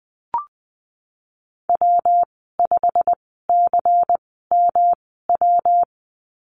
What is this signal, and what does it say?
Morse code, heard as a keyed tone, about 20 wpm, spaced as for 15 wpm, 700 Hz. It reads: W5CMW
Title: 154 Lesson 25g - Letter p - Call Signs 20wpm